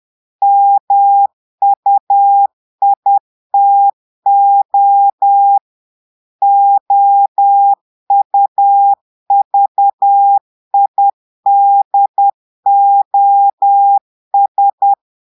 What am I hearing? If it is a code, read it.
Morse code, heard as a keyed tone, about 10 wpm, 800 Hz. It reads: MUITO OUVIDOS
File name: portugais texte en clair 2 vitesse 10 mots minutes